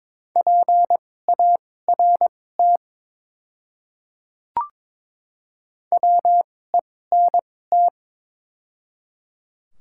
Morse code, heard as a keyed tone, about 22 wpm, spaced as for 16 wpm, 700 Hz. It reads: PART WENT
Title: 2x Character Spacing - Top 200 Words - 22wpm